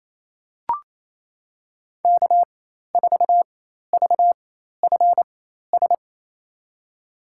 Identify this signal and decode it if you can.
Morse code, heard as a keyed tone, about 28 wpm, spaced as for 13 wpm, 700 Hz. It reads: K4VFS
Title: USA Call Signs - 5 Characters - 4x Character Spacing - 28wpm